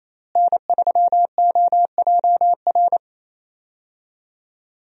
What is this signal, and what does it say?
Morse code, heard as a keyed tone, about 27 wpm, 700 Hz. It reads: N3OJR
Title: USA Call Signs - 5 Characters - Repeated 2x - 28wpm